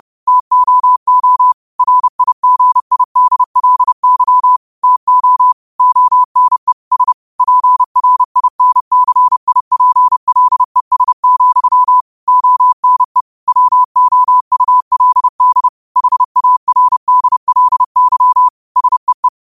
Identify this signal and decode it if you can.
Morse code, heard as a keyed tone, about 30 wpm, 1000 Hz. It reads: TOO RIGIDLY TO ONE S PRINCIPLES, ONE WOULD HARDLY SEE
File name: qotdfiles / 30wpm